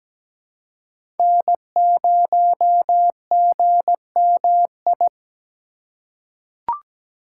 Morse code, heard as a keyed tone, about 17 wpm, 700 Hz. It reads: N0GMI
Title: USA Call Signs - 5 Characters - 4x Character Spacing - 17wpm